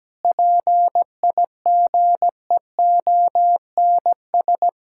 Morse code, heard as a keyed tone, about 17 wpm, 700 Hz. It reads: PIGEONS
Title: Pangram Challenge v2 - 17wpm